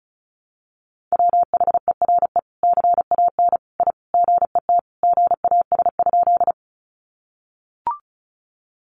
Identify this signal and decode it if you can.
Morse code, heard as a keyed tone, about 34 wpm, 700 Hz. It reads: WHERE CAN I GET GAS?
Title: Sentences from Top 1100 Words 35wpm